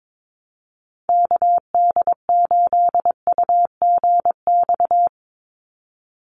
Morse code, heard as a keyed tone, about 22 wpm, 700 Hz. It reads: KD8UGX
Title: USA Call Signs 22wpm